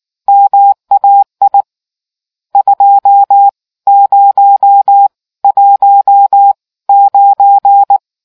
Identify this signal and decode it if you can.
Morse code, heard as a keyed tone, about 19 wpm, spaced as for 15 wpm, 800 Hz. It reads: MAI 2019